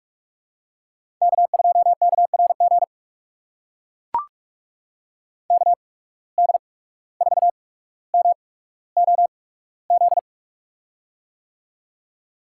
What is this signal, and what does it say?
Morse code, heard as a keyed tone, about 44 wpm, spaced as for 18 wpm, 690 Hz. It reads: K2KRG KD4MOZ
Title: USA Call Signs - 8x Character Spacing - 45wpm